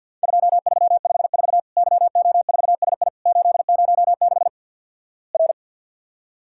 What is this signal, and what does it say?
Morse code, heard as a keyed tone, about 52 wpm, 700 Hz. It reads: 2354 YO4SI 806 R
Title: Worked All Europe DX Contest - QTC Exchange - 50wpm